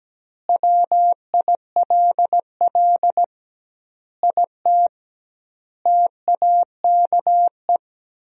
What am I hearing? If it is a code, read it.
Morse code, heard as a keyed tone, about 17 wpm, 700 Hz. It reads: WILL IT TAKE
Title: 2x Word-Spacing - Sentences from Top 200 Words - 17wpm